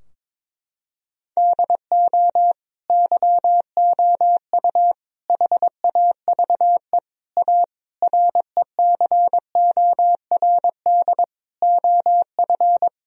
Morse code, heard as a keyed tone, about 22 wpm, 700 Hz. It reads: DO YOU HAVE A RECORD OF
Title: Sentences from Top 1000 Words - Review Part 3 - 22wpm